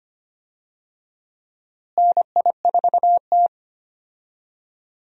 Morse code, heard as a keyed tone, about 25 wpm, 700 Hz. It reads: NI4T
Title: USA Call Signs - 4 Characters - Repeated 2x - 25wpm